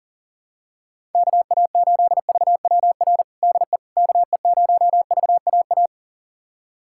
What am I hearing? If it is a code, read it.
Morse code, heard as a keyed tone, about 40 wpm, 700 Hz. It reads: KA8VWR DE KE0VAA
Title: USA Callsign DE Callsign - 6 Characters - 40wpm